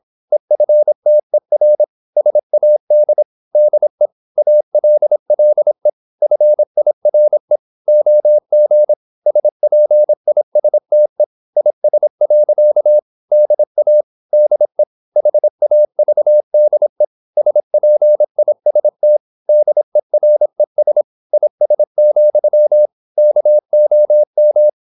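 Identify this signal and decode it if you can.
Morse code, heard as a keyed tone, about 26 wpm, 600 Hz. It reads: EFTER SAD DE ALLE FIRE OG SPISTE IS. DA DE HAVDE SPIST DERES IS, KOM
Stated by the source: High-speed (QRQ) Morse at 26 wpm.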